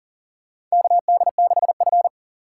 Morse code, heard as a keyed tone, about 40 wpm, 700 Hz. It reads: KD6F
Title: USA Call Signs - 4 Characters - 4x Character Spacing - 40wpm